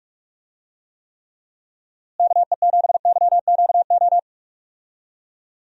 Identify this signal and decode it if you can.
Morse code, heard as a keyed tone, about 46 wpm, 700 Hz. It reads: KE7YQO
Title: USA Call Signs - 6 Characters - Repeated 2x - 45wpm